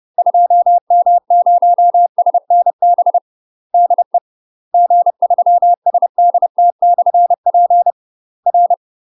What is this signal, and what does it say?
Morse code, heard as a keyed tone, about 30 wpm, 700 Hz. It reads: 2M0SNB DE G3SDT/P R
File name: UK MASTER QSOs TRAINING / 30WPM / UK MASTER QSOs 0011